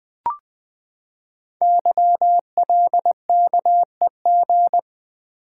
Morse code, heard as a keyed tone, about 20 wpm, 700 Hz. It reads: YLKEG
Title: Code Groups - 5 Letters - 20wpm